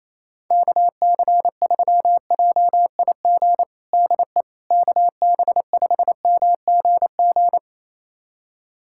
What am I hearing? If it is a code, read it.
Morse code, heard as a keyed tone, about 28 wpm, 700 Hz. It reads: KC3JIG DE KB5MGG
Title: USA Callsign DE Callsign 28wpm